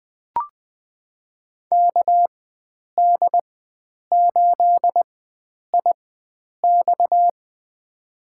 Morse code, heard as a keyed tone, about 21 wpm, spaced as for 9 wpm, 700 Hz. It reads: KD8IX
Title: USA Call Signs - 4x Character Spacing - 20wpm